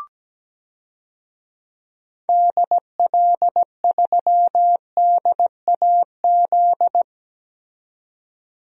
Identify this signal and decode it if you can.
Morse code, heard as a keyed tone, about 17 wpm, 700 Hz. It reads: DL3DAZ
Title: DXCC-100 Call Signs - 6 Characters - 17wpm